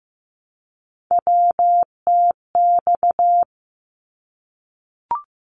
Morse code, heard as a keyed tone, about 15 wpm, 700 Hz. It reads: WTX
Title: ARRL Sections 15wpm